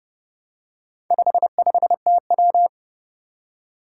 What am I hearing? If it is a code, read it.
Morse code, heard as a keyed tone, about 28 wpm, 700 Hz. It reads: 55TW
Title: Four Challenging Letter-Number Combinations 30wpm